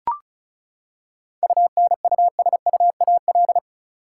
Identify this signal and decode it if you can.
Morse code, heard as a keyed tone, about 34 wpm, 700 Hz. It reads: UNUSUAL
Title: Seven Letter Words - 35wpm